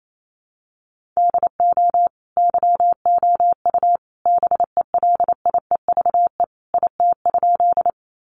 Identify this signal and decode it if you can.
Morse code, heard as a keyed tone, about 28 wpm, 700 Hz. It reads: DO YOU BELIEVE IT?
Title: Sentences from Top 1000 Words 28wpm